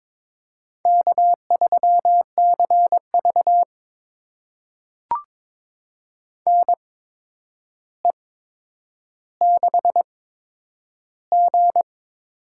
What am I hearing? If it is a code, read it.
Morse code, heard as a keyed tone, about 22 wpm, spaced as for 9 wpm, 700 Hz. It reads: K3CV NE6G
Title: USA Call Signs - 4 Characters - 8x Character Spacing - 22wpm